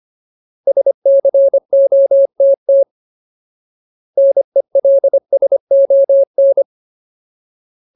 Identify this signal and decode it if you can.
Morse code, heard as a keyed tone, about 25 wpm, 550 Hz. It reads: SCOTT NELSON
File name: Ultimate Word List / 25WPM Ultimate